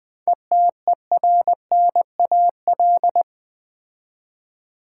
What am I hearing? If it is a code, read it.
Morse code, heard as a keyed tone, about 20 wpm, 700 Hz. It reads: ETERNAL
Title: 7 Letter Words - First 3 Letters Given - 20wpm